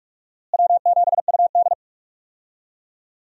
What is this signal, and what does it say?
Morse code, heard as a keyed tone, about 42 wpm, 700 Hz. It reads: W7UD
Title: USA Call Signs - 4 Characters - Repeated 2x - 45wpm